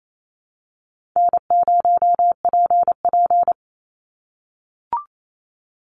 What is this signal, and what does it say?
Morse code, heard as a keyed tone, about 28 wpm, 700 Hz. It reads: N0PP
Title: Easy Call Signs - 28wpm